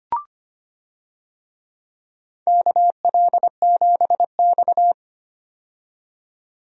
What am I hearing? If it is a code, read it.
Morse code, heard as a keyed tone, about 25 wpm, 700 Hz. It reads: KL7X
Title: DXCC-100 Call Signs - 4 Characters - 25wpm